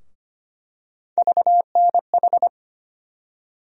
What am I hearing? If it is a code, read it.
Morse code, heard as a keyed tone, about 25 wpm, 700 Hz. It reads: VNH
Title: USA Call Sign Postfixes - 3 Letters 25wpm